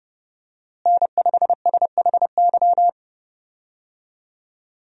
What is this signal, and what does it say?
Morse code, heard as a keyed tone, about 32 wpm, 700 Hz. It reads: N5SHY
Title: USA Call Signs - 5 Characters - Repeated 2x - 30wpm